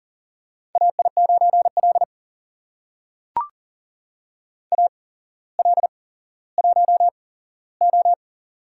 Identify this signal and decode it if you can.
Morse code, heard as a keyed tone, about 38 wpm, spaced as for 17 wpm, 700 Hz. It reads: AI9L AL1O
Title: USA Call Signs - 4 Characters - 8x Character Spacing - 40wpm